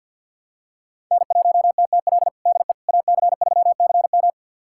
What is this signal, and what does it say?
Morse code, heard as a keyed tone, about 51 wpm, 700 Hz. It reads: N1TTL DE AC3CM
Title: USA Callsign DE Callsign - 5 Characters - 50wpm